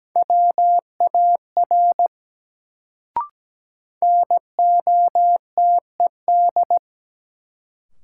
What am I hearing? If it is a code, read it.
Morse code, heard as a keyed tone, about 17 wpm, 700 Hz. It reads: WAR NOTED
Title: Sets of 5 Words - New Word Drill - 17wpm